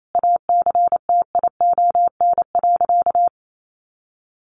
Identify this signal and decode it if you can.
Morse code, heard as a keyed tone, about 28 wpm, 700 Hz. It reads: ACTION.
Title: Sentences from Top 1300 Words 28wpm